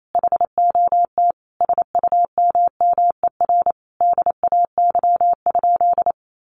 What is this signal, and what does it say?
Morse code, heard as a keyed tone, about 28 wpm, 700 Hz. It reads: HOT SUMMER DAY?
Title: Sentences from Top 1500 Words 28wpm